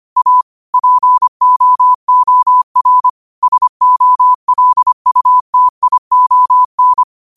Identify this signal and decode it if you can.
Morse code, heard as a keyed tone, about 25 wpm, 1000 Hz. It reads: A POOR SOLUTION
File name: qotdfiles / 25wpm